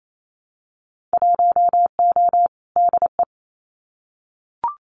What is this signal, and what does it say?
Morse code, heard as a keyed tone, about 28 wpm, 700 Hz. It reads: 1O DE
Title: Winter Field Day Exchanges 28wpm